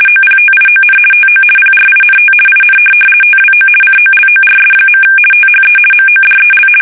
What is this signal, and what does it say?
RTTY, about 50 baud. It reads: OPTION TO PRINT IN TWO COLORS. THE MODEL
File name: fsk850-50B / teletype-corp